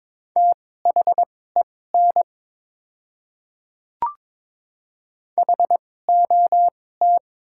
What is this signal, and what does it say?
Morse code, heard as a keyed tone, about 22 wpm, spaced as for 16 wpm, 700 Hz. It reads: THEN HOT
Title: 2x Character Spacing - Top 300 Words - 22wpm